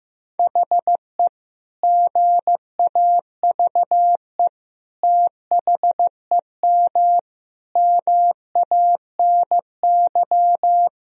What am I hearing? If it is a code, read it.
Morse code, heard as a keyed tone, about 15 wpm, 700 Hz. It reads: HE GAVE THEM MANY
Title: Sentences from Top 1000 Words - Review Part 3 - 15wpm